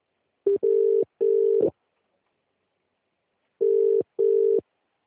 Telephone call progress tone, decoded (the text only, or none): ringback tone